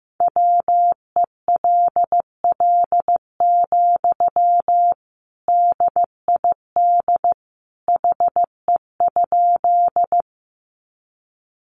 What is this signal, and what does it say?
Morse code, heard as a keyed tone, about 15 wpm, 700 Hz. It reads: WELL, DID HE?
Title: Sentences from Top 200 Words 15wpm